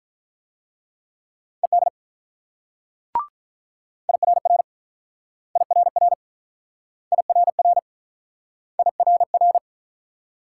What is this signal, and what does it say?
Morse code, heard as a keyed tone, about 43 wpm, 700 Hz. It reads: ED IRR IRR IRR IRR
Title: Speed-Racing Multi-letter Phonemes - 35wpm